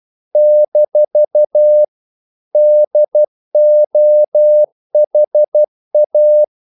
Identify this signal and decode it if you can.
Morse code, heard as a keyed tone, about 12 wpm, 600 Hz. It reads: - DOHA
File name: CW / Practice / NationalCapitals12